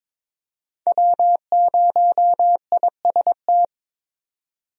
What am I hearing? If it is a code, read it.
Morse code, heard as a keyed tone, about 22 wpm, 700 Hz. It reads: W0IST
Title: USA Call Signs - 5 Characters - 22wpm